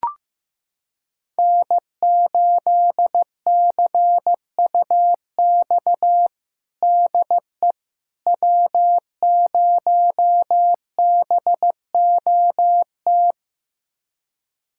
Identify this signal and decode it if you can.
Morse code, heard as a keyed tone, about 15 wpm, 700 Hz. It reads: N8CUX DE W0BOT
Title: USA Callsign DE Callsign - 5 Characters - 15wpm